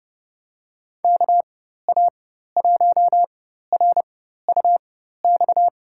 Morse code, heard as a keyed tone, about 32 wpm, spaced as for 14 wpm, 700 Hz. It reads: KA1RUX
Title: USA Call Signs - 4x Character Spacing - 30wpm